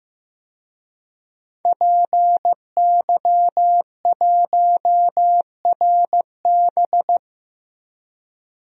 Morse code, heard as a keyed tone, about 15 wpm, 700 Hz. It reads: PY1RB
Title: DXCC-100 Call Signs - 5 Characters - Repeated 2x - 15wpm